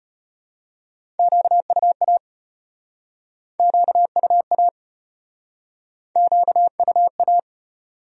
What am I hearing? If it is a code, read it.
Morse code, heard as a keyed tone, about 37 wpm, 700 Hz. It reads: QUA QUA QUA